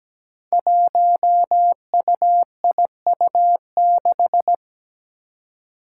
Morse code, heard as a keyed tone, about 17 wpm, 700 Hz. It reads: 1UIU6
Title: Five Challenging Letter-Number Combinations 17wpm